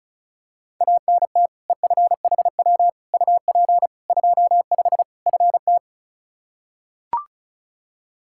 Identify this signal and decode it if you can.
Morse code, heard as a keyed tone, about 34 wpm, 700 Hz. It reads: ANT EFHW UP 25 FT
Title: Practice QSO - Exchange 2 - Info - 35wpm